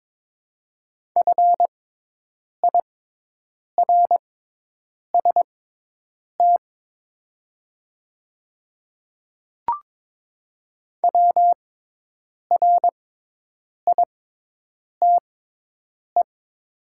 Morse code, heard as a keyed tone, about 22 wpm, spaced as for 8 wpm, 700 Hz. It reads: FIRST WRITE
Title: 6x Character Spacing - Top 100 Words - 22wpm